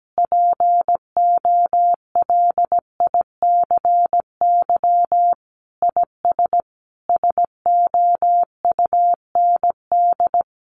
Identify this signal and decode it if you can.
Morse code, heard as a keyed tone, about 17 wpm, 700 Hz. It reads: POLICY IS SOUND
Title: Sentences from Top 1400 Words 17wpm